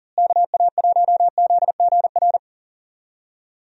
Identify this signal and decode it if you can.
Morse code, heard as a keyed tone, about 40 wpm, 700 Hz. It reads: KA1ZGR
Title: USA Call Signs - 6 Characters - Repeated 3x - 40wpm